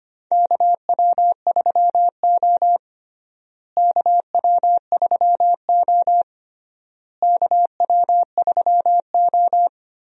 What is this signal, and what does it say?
Morse code, heard as a keyed tone, about 25 wpm, 700 Hz. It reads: KW3O KW3O KW3O
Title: USA Call Signs - 4 Characters - Repeated 3x - 25wpm